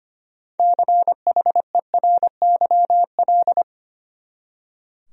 Morse code, heard as a keyed tone, about 25 wpm, 700 Hz. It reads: CHERYL
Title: USA Common Names - 25wpm